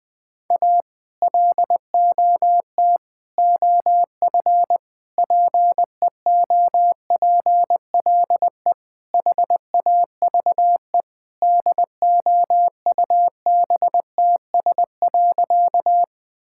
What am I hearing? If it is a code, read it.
Morse code, heard as a keyed tone, about 21 wpm, 700 Hz. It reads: A LOT OF PEOPLE HAVE DOUBTS.
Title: Sentences with 5-Word Phrases - Spoken Beginning - 20wpm